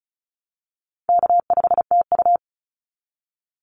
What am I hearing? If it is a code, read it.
Morse code, heard as a keyed tone, about 34 wpm, 700 Hz. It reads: K5TU
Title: Call Signs Repeated 2x - 35wpm